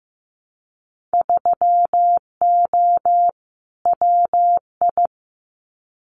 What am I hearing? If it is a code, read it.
Morse code, heard as a keyed tone, about 15 wpm, 700 Hz. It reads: 3O WI
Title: Winter Field Day Exchanges 15wpm